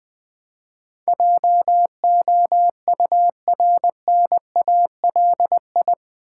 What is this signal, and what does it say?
Morse code, heard as a keyed tone, about 20 wpm, 700 Hz. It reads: JOURNALI
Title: Build A Word - 4 Letters At A Time - 10-Letter Words - 20wpm